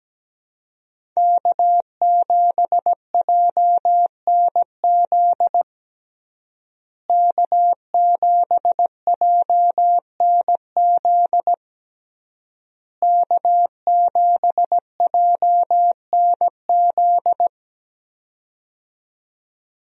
Morse code, heard as a keyed tone, about 17 wpm, 700 Hz. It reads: K7JNZ K7JNZ K7JNZ
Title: USA Call Signs - 5 Characters - Repeated 3x - 17wpm